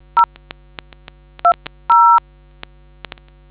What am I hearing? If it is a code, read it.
DTMF, heard as telephone touch tones, tones of about 100 ms. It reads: 0 20